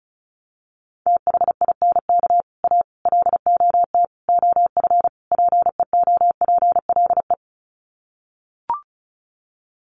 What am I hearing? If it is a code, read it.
Morse code, heard as a keyed tone, about 34 wpm, 700 Hz. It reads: THINK A LOT OF PEOPLE
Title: Sets of 5 Words 35wpm